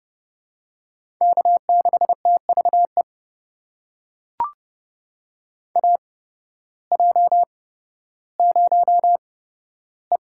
Morse code, heard as a keyed tone, about 30 wpm, spaced as for 14 wpm, 700 Hz. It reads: K6TVE AJ0E
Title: USA Call Signs - 8x Character Spacing - 30wpm